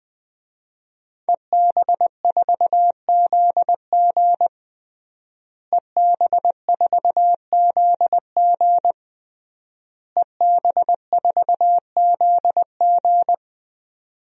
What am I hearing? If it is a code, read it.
Morse code, heard as a keyed tone, about 19 wpm, 700 Hz. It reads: EB4ZG EB4ZG EB4ZG
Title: DXCC-100 Call Signs - 5 Characters - Repeated 3x - 20wpm